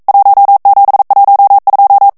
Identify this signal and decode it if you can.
Morse code, heard as a keyed tone, about 42 wpm, 800 Hz. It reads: 1712
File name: ITALIEN texte clair 1 vitesse 40 mots minute